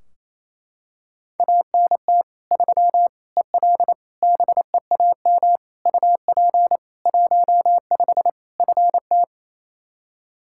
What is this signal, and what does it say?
Morse code, heard as a keyed tone, about 28 wpm, 700 Hz. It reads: ANT 3 EL BEAM UP 15 FT
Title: Practice QSO - Exchange 2 - Info - 28wpm